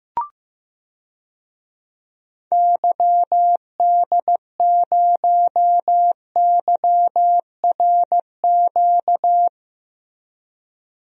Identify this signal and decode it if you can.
Morse code, heard as a keyed tone, about 15 wpm, 700 Hz. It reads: YD0YRQ
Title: DXCC-100 Call Signs - 6 Characters - Repeated 3x - 15wpm